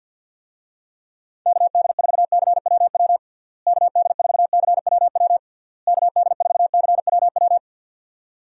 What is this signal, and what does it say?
Morse code, heard as a keyed tone, about 52 wpm, 700 Hz. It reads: KD4CWW KD4CWW KD4CWW
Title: USA Call Signs - 6 Characters - Repeated 3x - 50wpm